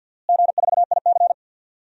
Morse code, heard as a keyed tone, about 52 wpm, 700 Hz. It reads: G4IC
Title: DXCC-100 Call Signs - 4 Characters - Repeated 3x - 50wpm